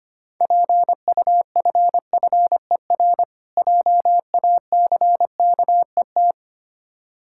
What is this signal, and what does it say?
Morse code, heard as a keyed tone, about 25 wpm, 690 Hz. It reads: PUFFER JACKET
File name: practice / 25wpm / Difficult Words with Hints - 25wpm